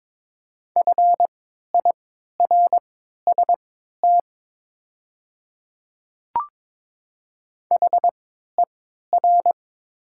Morse code, heard as a keyed tone, about 22 wpm, spaced as for 12 wpm, 700 Hz. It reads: FIRST HER
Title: 3x Character Spacing - Top 200 Words - 22wpm